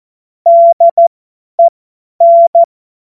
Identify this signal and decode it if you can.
Morse code, heard as a keyed tone, about 14 wpm, spaced as for 10 wpm, 680 Hz. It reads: DEN
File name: Morsekurs / Witze / Witz 25. Oktober 2022 50 F70